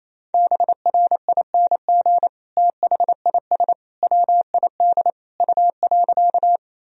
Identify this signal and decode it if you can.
Morse code, heard as a keyed tone, about 27 wpm, 700 Hz. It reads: BRING THIS WID U.
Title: Sentences from Top 500 Words and Abbreviations 28wpm